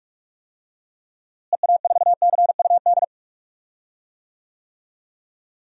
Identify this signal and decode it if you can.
Morse code, heard as a keyed tone, about 41 wpm, 700 Hz. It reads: EA4CUD
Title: DXCC-100 Call Signs - 6 Characters - Repeated 2x - 45wpm